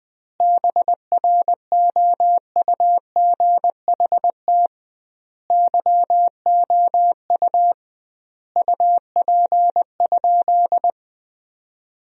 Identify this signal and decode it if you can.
Morse code, heard as a keyed tone, about 21 wpm, 700 Hz. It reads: BROUGHT YOU UP?
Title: 2x Word-Spacing - Sentences from Top 500 Words - 20wpm